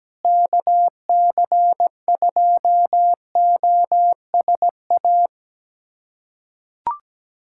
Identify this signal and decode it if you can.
Morse code, heard as a keyed tone, about 17 wpm, 700 Hz. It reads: KC2OSA
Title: USA Callsign DE Callsign 17wpm